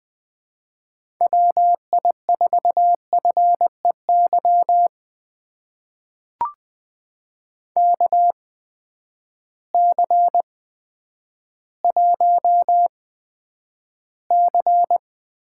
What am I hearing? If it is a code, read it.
Morse code, heard as a keyed tone, about 20 wpm, spaced as for 10 wpm, 700 Hz. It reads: WI4FEY KC1C
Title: USA Call Signs - 6 Characters - 8x Character Spacing - 20wpm